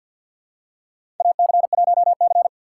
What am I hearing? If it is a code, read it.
Morse code, heard as a keyed tone, about 52 wpm, 700 Hz. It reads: AC1C
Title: USA Call Signs - 2x Character Spacing - 50wpm